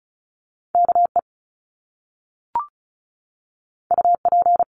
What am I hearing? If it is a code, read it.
Morse code, heard as a keyed tone, about 36 wpm, 700 Hz. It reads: K E U P
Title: Top Two Letter Combinations in Words 35wpm